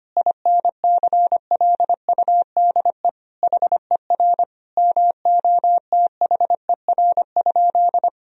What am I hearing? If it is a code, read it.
Morse code, heard as a keyed tone, about 25 wpm, 700 Hz. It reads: INCLUDE HER MOTHER?
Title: Sentences from Top 1000 Words - Review Part 3 - 25wpm